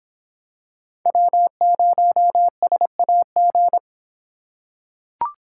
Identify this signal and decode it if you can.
Morse code, heard as a keyed tone, about 26 wpm, 700 Hz. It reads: W0SAG